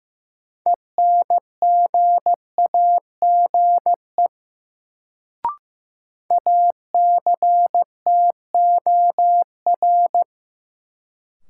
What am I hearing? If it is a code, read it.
Morse code, heard as a keyed tone, about 15 wpm, 700 Hz. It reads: ENGAGE ACTOR
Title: Sentences from Top 1500 Words - New Word Drill Alt - 15wpm